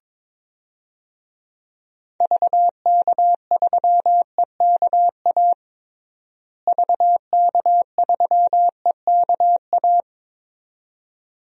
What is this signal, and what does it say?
Morse code, heard as a keyed tone, about 21 wpm, 700 Hz. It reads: VK3EKA VK3EKA
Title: DXCC-100 Call Signs - 6 Characters - Repeated 2x - 22wpm